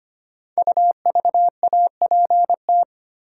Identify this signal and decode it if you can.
Morse code, heard as a keyed tone, about 25 wpm, 700 Hz. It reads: UVAPT
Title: Code Groups - 5 Letter:Numbers - 25wpm